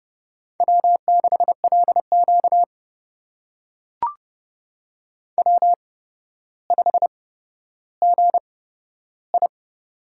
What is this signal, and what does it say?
Morse code, heard as a keyed tone, about 29 wpm, spaced as for 13 wpm, 700 Hz. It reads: W6LQ W5GI
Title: USA Call Signs - 4 Characters - 8x Character Spacing - 30wpm